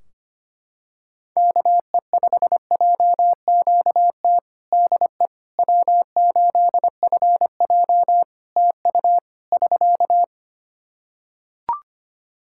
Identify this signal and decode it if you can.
Morse code, heard as a keyed tone, about 25 wpm, 700 Hz. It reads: KE5JQT DE W8FJ TU <SK>
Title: Practice QSO - Exchange 4 - Ending - 25wpm